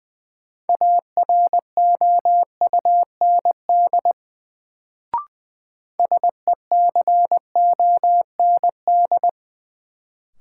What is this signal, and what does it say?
Morse code, heard as a keyed tone, about 21 wpm, 700 Hz. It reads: AROUND SECOND
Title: Sets of 2 Words - New Word Drill - 20wpm